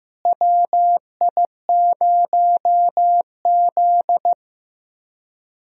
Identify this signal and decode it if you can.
Morse code, heard as a keyed tone, about 15 wpm, 700 Hz. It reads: WI0Z
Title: USA Call Signs - 4 Characters - Repeated 3x - 15wpm